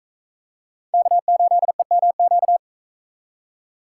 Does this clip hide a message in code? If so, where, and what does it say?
Yes: Morse code from 0.9 to 2.6 s, heard as a keyed tone, about 41 wpm, 700 Hz. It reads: K8EMQ